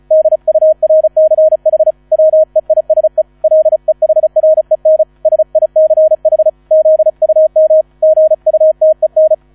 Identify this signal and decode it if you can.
Morse code, heard as a keyed tone, about 34 wpm, 630 Hz. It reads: DURCH WEISE LEHREN SICH ZUM GUTEN